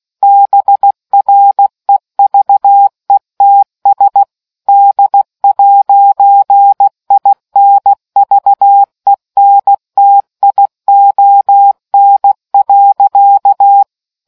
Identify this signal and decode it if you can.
Morse code, heard as a keyed tone, about 16 wpm, 800 Hz. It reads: BREVETS D'INVENTION.